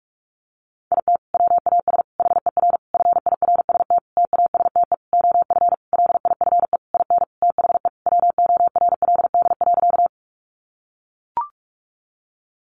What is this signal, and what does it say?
Morse code, heard as a keyed tone, about 44 wpm, 700 Hz. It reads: IT WAS HER FIRST TASTE OF LIFE IN THE WORLD.
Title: Sentences from Top 1100 Words 45wpm